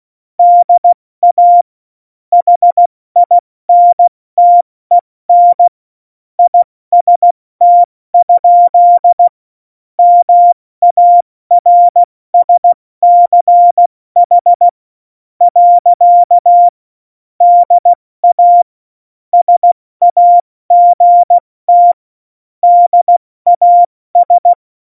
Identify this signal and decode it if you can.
Morse code, heard as a keyed tone, about 16 wpm, spaced as for 14 wpm, 700 Hz. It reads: DA HINTEN IST? MARSCH. DA SAGT DAS
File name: Morsekurs